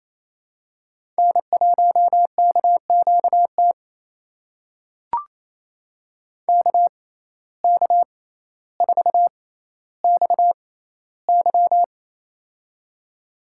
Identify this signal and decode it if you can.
Morse code, heard as a keyed tone, about 28 wpm, spaced as for 14 wpm, 700 Hz. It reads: N1KQT KK4XY
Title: USA Call Signs - 5 Characters - 6x Character Spacing - 28wpm